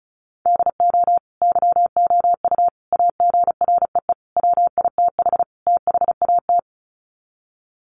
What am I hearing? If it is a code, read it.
Morse code, heard as a keyed tone, about 34 wpm, 700 Hz. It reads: DO YOU AGREE WITH THAT
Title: Sets of 5 Words Encore 35wpm